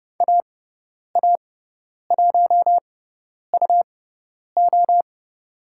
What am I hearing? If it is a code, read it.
Morse code, heard as a keyed tone, about 30 wpm, spaced as for 10 wpm, 700 Hz. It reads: AA1UO
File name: Lesson 17g - Letter u - Call Signs 30wpm